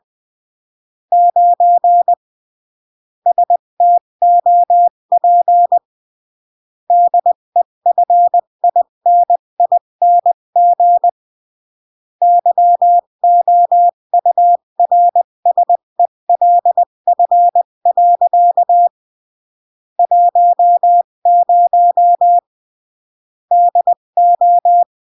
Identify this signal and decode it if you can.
Morse code, heard as a keyed tone, about 20 wpm, 700 Hz. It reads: 9 STOP DEFINING YOURSELF. 10 DO
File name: CW / jodorowsky / 82 commandments 20-15